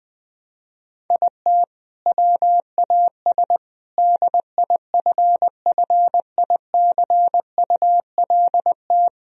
Morse code, heard as a keyed tone, about 21 wpm, 700 Hz. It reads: IT WAS DIFFICULT
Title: Sentences from Top 1000 Words - Review Part 1 - 20wpm